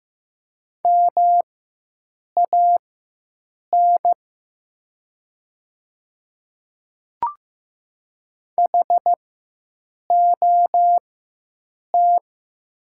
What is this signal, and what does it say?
Morse code, heard as a keyed tone, about 15 wpm, spaced as for 7 wpm, 700 Hz. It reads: MAN HOT
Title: 4x Character Spacing - Top 200 Words - 15wpm